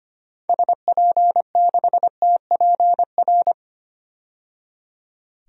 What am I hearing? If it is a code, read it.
Morse code, heard as a keyed tone, about 25 wpm, 700 Hz. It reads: SP6TPR
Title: DXCC-100 Call Signs - 6 Characters - 25wpm